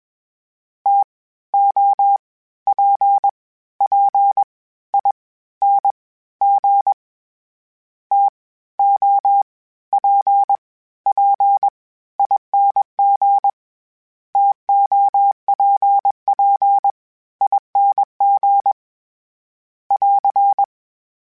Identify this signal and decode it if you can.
Morse code, heard as a keyed tone, about 21 wpm, 800 Hz. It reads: T O P P I N G T O P P ING TOPP ING <AR>
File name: Variable Spacing / Variable Spacing-mp3s / TOs @20